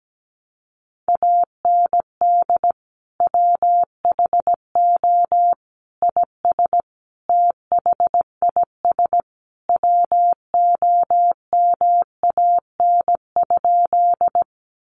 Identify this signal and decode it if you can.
Morse code, heard as a keyed tone, about 17 wpm, 700 Hz. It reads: AND WHO IS THIS WOMAN?
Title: Sentences from Top 1100 Words 17wpm